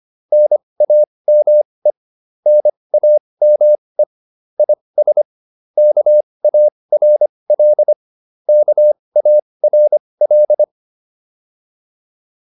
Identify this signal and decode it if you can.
Morse code, heard as a keyed tone, about 25 wpm, spaced as for 20 wpm, 600 Hz. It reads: NAME NAME IS KARL KARL